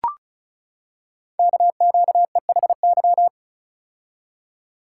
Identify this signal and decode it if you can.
Morse code, heard as a keyed tone, about 36 wpm, 700 Hz. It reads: KQEHY
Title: Code Groups - 5 Letters - 35wpm